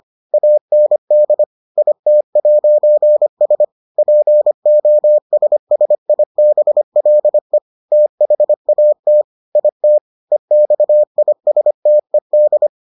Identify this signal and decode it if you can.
Morse code, heard as a keyed tone, about 25 wpm, 600 Hz. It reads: AND IT'S POSSIBLE THAT IT EXISTED